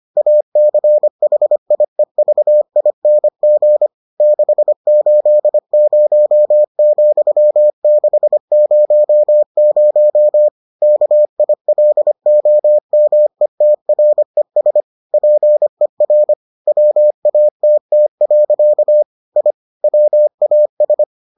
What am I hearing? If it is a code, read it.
Morse code, heard as a keyed tone, about 25 wpm, 600 Hz. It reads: ACHIEVING 680,600 KILOMETRES PER WATT. I WAS